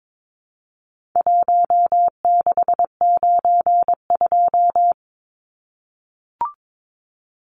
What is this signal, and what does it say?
Morse code, heard as a keyed tone, about 22 wpm, 700 Hz. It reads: 1692
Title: Numbers - 4-Digit 22wpm